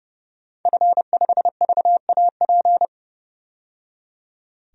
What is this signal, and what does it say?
Morse code, heard as a keyed tone, about 30 wpm, 700 Hz. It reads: F5VAP
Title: DXCC-100 Call Signs - 5 Characters - 30wpm